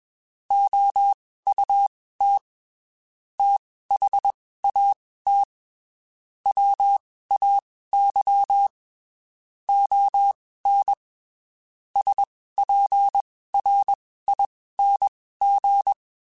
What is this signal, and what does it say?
Morse code, heard as a keyed tone, about 21 wpm, spaced as for 15 wpm, 800 Hz. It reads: OUT THAT WAY ON SPRING
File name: (spaced) / QSOs Spaced-mp3s / WA7QOX 1 @20 sp